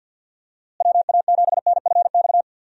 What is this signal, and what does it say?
Morse code, heard as a keyed tone, about 46 wpm, 700 Hz. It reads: WA7NFX
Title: USA Call Signs - 6 Characters - Repeated 2x - 50wpm